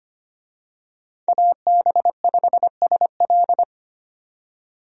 Morse code, heard as a keyed tone, about 25 wpm, 700 Hz. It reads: AB5SL
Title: USA Call Signs - 5 Characters - 25wpm